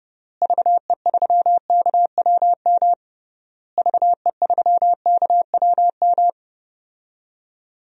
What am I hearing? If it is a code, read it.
Morse code, heard as a keyed tone, about 30 wpm, 700 Hz. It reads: VE3KWM VE3KWM
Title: DXCC-100 Call Signs - 6 Characters - Repeated 3x - 30wpm